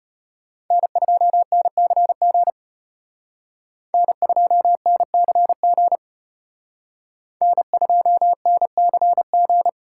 Morse code, heard as a keyed tone, about 33 wpm, 700 Hz. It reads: N2NCG N2NCG N2NCG